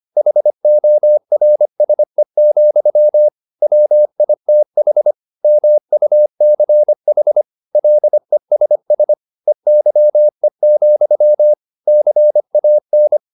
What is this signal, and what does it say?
Morse code, heard as a keyed tone, about 25 wpm, 600 Hz. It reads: HORSE, WITH MUCH LESS EYE, CAN
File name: the old man and the sea part 1 25wpm